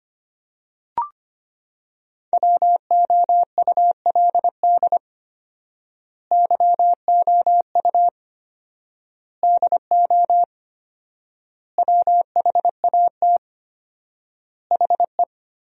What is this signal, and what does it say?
Morse code, heard as a keyed tone, about 25 wpm, 700 Hz. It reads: WOULD YOU DO WHAT HE
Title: 4x Word-Spacing - Sentences from Top 100 Words - 25wpm